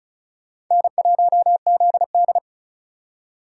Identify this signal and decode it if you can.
Morse code, heard as a keyed tone, about 45 wpm, 700 Hz. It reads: N1ZD